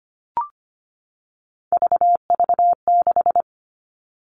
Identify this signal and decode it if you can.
Morse code, heard as a keyed tone, about 25 wpm, 700 Hz. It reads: VV6
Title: All Cut Numbers - 3-Digit 25wpm